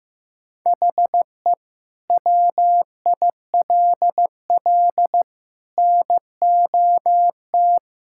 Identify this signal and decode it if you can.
Morse code, heard as a keyed tone, about 15 wpm, 700 Hz. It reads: HE WILL NOT
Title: Sentences from Top 1000 Words - Review Part 6 - 15wpm